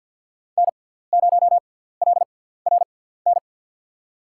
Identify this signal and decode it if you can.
Morse code, heard as a keyed tone, about 52 wpm, spaced as for 17 wpm, 700 Hz. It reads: N0LRN
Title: USA Call Signs - 5 Characters - 6x Character Spacing - 50wpm